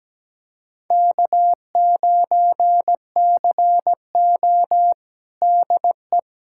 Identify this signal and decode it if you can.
Morse code, heard as a keyed tone, about 17 wpm, 700 Hz. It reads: K9CO DE
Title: USA Callsign DE Callsign - 4 Characters - 17wpm